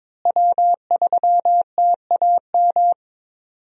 Morse code, heard as a keyed tone, about 22 wpm, 700 Hz. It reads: W3TAM
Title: USA Call Signs - 5 Characters - Repeated 3x - 22wpm